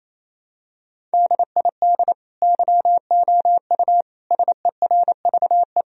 Morse code, heard as a keyed tone, about 28 wpm, 700 Hz. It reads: DID YOU SERVE
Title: Sentences from Top 1000 Words - Review Part 4 - 28wpm